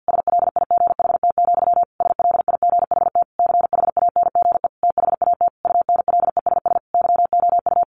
Morse code, heard as a keyed tone, about 50 wpm, 700 Hz. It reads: SLIGHT, SLIGHT CHANGE THAT UNLESS YOU
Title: A Sound Of Thunder 50wpm